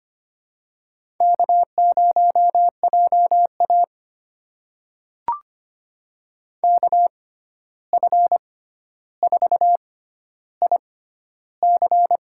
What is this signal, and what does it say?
Morse code, heard as a keyed tone, about 25 wpm, spaced as for 12 wpm, 700 Hz. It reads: K0JA KF4IC
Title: USA Call Signs - 6x Character Spacing - 25wpm